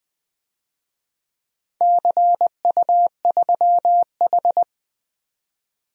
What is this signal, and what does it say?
Morse code, heard as a keyed tone, about 20 wpm, 700 Hz. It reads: CU3H
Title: DXCC-100 Call Signs - 4 Characters - 20wpm